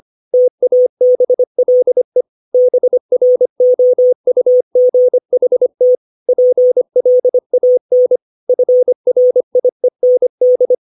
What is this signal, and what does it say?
Morse code, heard as a keyed tone, about 25 wpm, 500 Hz. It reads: TABLE BROUGHT PLAN FRIEND
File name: 632 words / 632 Words 25WPM